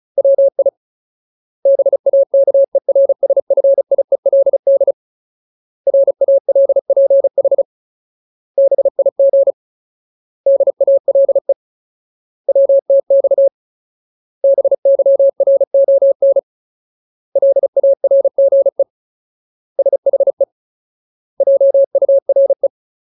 Morse code, heard as a keyed tone, about 34 wpm, 550 Hz. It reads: WI BAKERSFIELD RALPH BIG DALE WTX BYRON LARGE SHE JURE
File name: Ultimate Word List / 35WPM Ultimate